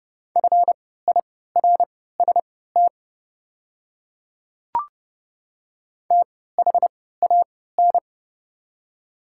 Morse code, heard as a keyed tone, about 30 wpm, spaced as for 17 wpm, 700 Hz. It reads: FIRST THAN
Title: 3x Character Spacing - Top 100 Words - 30wpm